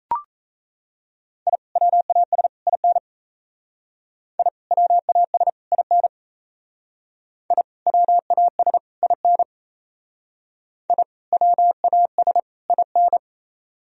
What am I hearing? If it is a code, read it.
Morse code, heard as a keyed tone, about 36 wpm, 700 Hz. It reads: I WAS IN I WAS IN I WAS IN I WAS IN